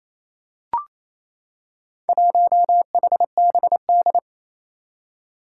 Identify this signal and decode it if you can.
Morse code, heard as a keyed tone, about 27 wpm, 700 Hz. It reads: 1HBD
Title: Four Challenging Letter-Number Combinations 28wpm